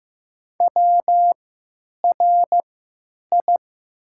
Morse code, heard as a keyed tone, about 15 wpm, spaced as for 9 wpm, 700 Hz. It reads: WRI
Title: 3x Character Spacing - Top 100 Words - 15wpm